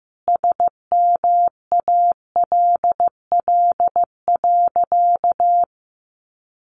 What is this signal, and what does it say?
Morse code, heard as a keyed tone, about 14 wpm, 700 Hz. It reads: SMALL.
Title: Sentences from Top 1400 Words 15wpm